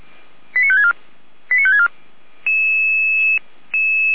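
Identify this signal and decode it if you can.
Five-tone selective call (ZVEI), heard as quick runs of five tones, tones of about 70 ms. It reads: 89564 89564
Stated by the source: Five-tone ZVEI selective call 89025, received on SDR.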